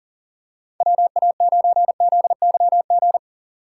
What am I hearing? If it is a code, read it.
Morse code, heard as a keyed tone, about 40 wpm, 700 Hz. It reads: WA9ZYG
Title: USA Call Signs - 8x Character Spacing - 40wpm